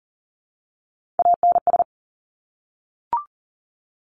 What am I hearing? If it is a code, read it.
Morse code, heard as a keyed tone, about 40 wpm, 700 Hz. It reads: ANS
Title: Top Three Letter Combinations in Words 40wpm